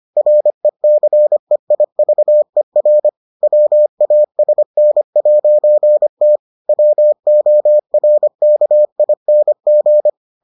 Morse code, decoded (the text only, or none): RECEIVER WASN'T WORKING